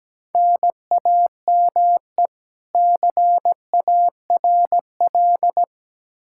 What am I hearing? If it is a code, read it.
Morse code, heard as a keyed tone, about 17 wpm, 700 Hz. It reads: NAME CARL
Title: Practice QSO - Exchange 1 - Beginning - 17wpm